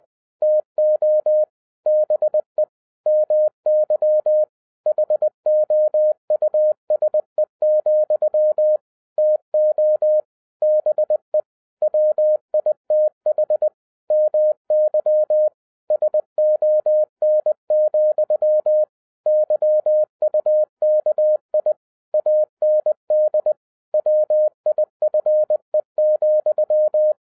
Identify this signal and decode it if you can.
Morse code, heard as a keyed tone, about 20 wpm, 610 Hz. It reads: TO BE MY HOUSE, TO BE WITH MY SON, YUKI AND WIFE,
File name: 2011-07-mp3 / n58 BACK FROM A BIG ACCIDENT 20wpm